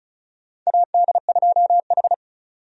Morse code, heard as a keyed tone, about 36 wpm, 700 Hz. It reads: AD2H
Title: USA Call Signs - 4 Characters - 35wpm